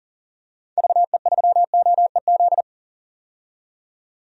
Morse code, heard as a keyed tone, about 40 wpm, 700 Hz. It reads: VE3OEZ
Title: DXCC-100 Call Signs - 6 Characters - 40wpm